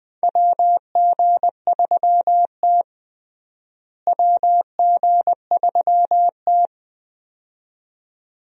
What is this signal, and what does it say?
Morse code, heard as a keyed tone, about 20 wpm, 700 Hz. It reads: WG3T WG3T
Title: USA Call Signs - 4 Characters - Repeated 2x - 20wpm